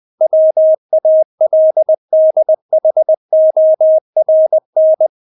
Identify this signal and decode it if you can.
Morse code, heard as a keyed tone, about 20 wpm, 630 Hz. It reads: WALDHORN
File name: Witz1 20wpm